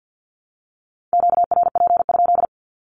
Morse code, heard as a keyed tone, about 49 wpm, 700 Hz. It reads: QRP?
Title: Q-Codes Repeated 3x 50wpm